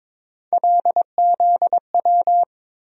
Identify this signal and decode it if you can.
Morse code, heard as a keyed tone, about 21 wpm, 700 Hz. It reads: LZW
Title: USA Call Sign Postfixes - 3 Letters 22wpm